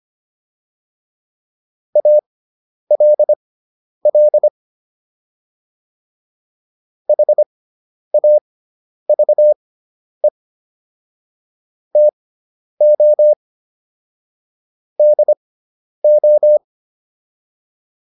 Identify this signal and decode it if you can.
Morse code, heard as a keyed tone, about 25 wpm, spaced as for 10 wpm, 600 Hz. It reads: ALL HAVE TO DO
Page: intermediate-practice-files